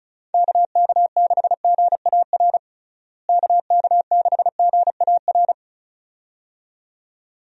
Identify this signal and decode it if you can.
Morse code, heard as a keyed tone, about 36 wpm, 700 Hz. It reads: KK6GAR KK6GAR
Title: USA Call Signs - 6 Characters - Repeated 2x - 35wpm